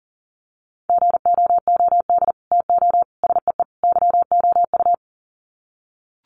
Morse code, heard as a keyed tone, about 40 wpm, 700 Hz. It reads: GOOD TO SEE YOU
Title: Sets of 4 Words 40wpm